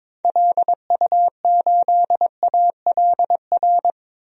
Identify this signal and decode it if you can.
Morse code, heard as a keyed tone, about 22 wpm, 700 Hz. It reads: LU8ALR
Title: DXCC-100 Call Signs - 6 Characters - 22wpm